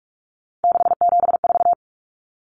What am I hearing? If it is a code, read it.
Morse code, heard as a keyed tone, about 44 wpm, 700 Hz. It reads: 674
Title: Common Cut Numbers - 3-Digit 45wpm